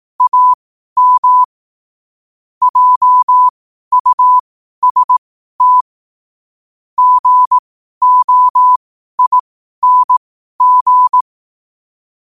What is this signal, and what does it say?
Morse code, heard as a keyed tone, about 18 wpm, spaced as for 12 wpm, 1000 Hz. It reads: AM JUST GOING
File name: qotdfiles / 12wpm